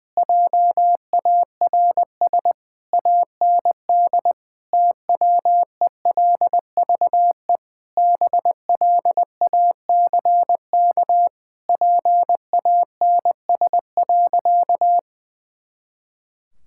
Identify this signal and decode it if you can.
Morse code, heard as a keyed tone, about 19 wpm, 700 Hz. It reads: JARS AND TWELVE BLACK PANS.
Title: Pangram Challenge v2 - 20wpm